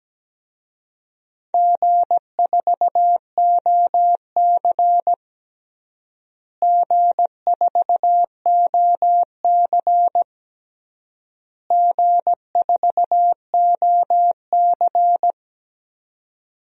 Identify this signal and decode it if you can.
Morse code, heard as a keyed tone, about 17 wpm, 700 Hz. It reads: G4OC G4OC G4OC
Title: DXCC-100 Call Signs - 4 Characters - Repeated 3x - 17wpm